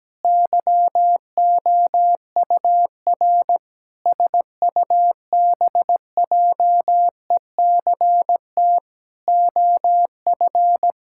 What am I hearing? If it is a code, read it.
Morse code, heard as a keyed tone, about 17 wpm, 700 Hz. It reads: YOUR SUBJECT OF
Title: Sentences from Top 1000 Words - Review Part 5 - 17wpm